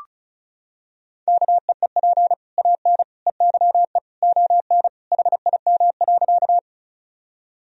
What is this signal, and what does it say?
Morse code, heard as a keyed tone, about 36 wpm, 700 Hz. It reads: KEEP AN EYE ON HIM.
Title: Sentences from Top 1000 Words - Review Part 1 - 35wpm